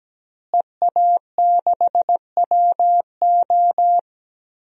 Morse code, heard as a keyed tone, about 17 wpm, 700 Hz. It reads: EA6WO
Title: DXCC-100 Call Signs - 5 Characters - Repeated 3x - 17wpm